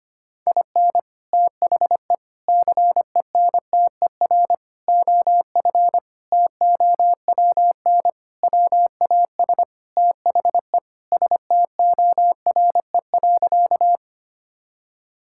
Morse code, heard as a keyed tone, about 25 wpm, 700 Hz. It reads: IN THE CENTER OF TOWN WAS THE STORE.
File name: practice / 25wpm / Sentences with 4-Word Phrases - 25wpm